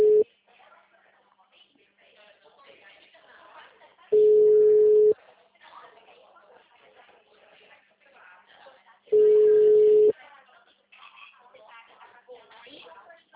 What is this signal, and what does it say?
Ringback tone (Europe, 425 Hz), with speech.